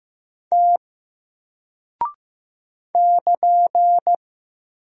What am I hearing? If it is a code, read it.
Morse code, heard as a keyed tone, about 15 wpm, 700 Hz. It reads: T <KN>
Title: Single Character (Letter:Number:Punctuation:Prosign) 15wpm